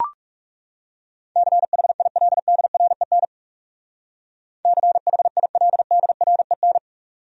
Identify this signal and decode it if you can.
Morse code, heard as a keyed tone, about 40 wpm, 700 Hz. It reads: CHILDREN CHILDREN